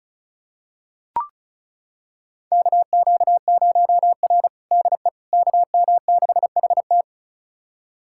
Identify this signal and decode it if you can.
Morse code, heard as a keyed tone, about 36 wpm, 700 Hz. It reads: KQ0R DE KM6HT
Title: USA Callsign DE Callsign 35wpm